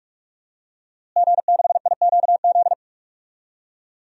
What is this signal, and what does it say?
Morse code, heard as a keyed tone, about 46 wpm, 700 Hz. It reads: G6IQZ
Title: DXCC-100 Call Signs - 5 Characters - Repeated 2x - 45wpm